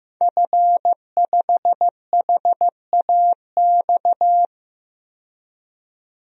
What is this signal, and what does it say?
Morse code, heard as a keyed tone, about 15 wpm, 700 Hz. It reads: F5HAX
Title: DXCC-100 Call Signs - 5 Characters - 15wpm